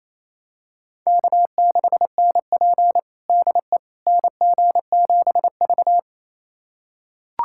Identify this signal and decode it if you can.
Morse code, heard as a keyed tone, about 28 wpm, 700 Hz. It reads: K6NP DE NG7V
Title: USA Callsign DE Callsign - 4 Characters - 28wpm